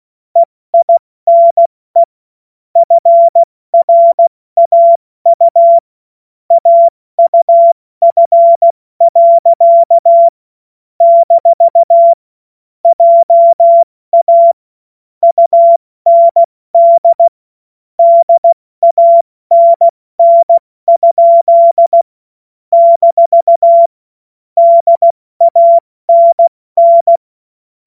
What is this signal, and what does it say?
Morse code, heard as a keyed tone, about 16 wpm, 680 Hz. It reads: EINE FRAU AUF. - JA UND DANN? - DANN
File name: Morsekurs / Witze / Witz 12 März 2024 70 F80